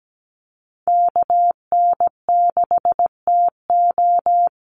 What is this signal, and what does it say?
Morse code, heard as a keyed tone, about 17 wpm, 700 Hz. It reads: KN6TO
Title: USA Call Signs 17wpm